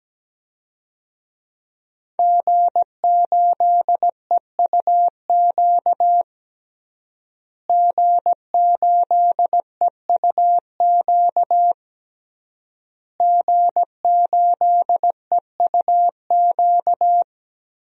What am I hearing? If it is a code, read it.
Morse code, heard as a keyed tone, about 17 wpm, 700 Hz. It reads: G8EUQ G8EUQ G8EUQ
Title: DXCC-100 Call Signs - 5 Characters - Repeated 3x - 17wpm